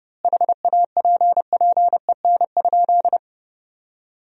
Morse code, heard as a keyed tone, about 30 wpm, 700 Hz. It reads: HAPPEN?
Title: 2x Word-Spacing - Sentences from Top 400 Words - 30wpm